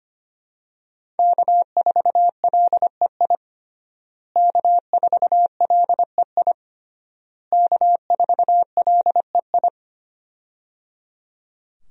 Morse code, heard as a keyed tone, about 25 wpm, 700 Hz. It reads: K4LEI K4LEI K4LEI
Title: USA Call Signs - 5 Characters - Repeated 3x - 25wpm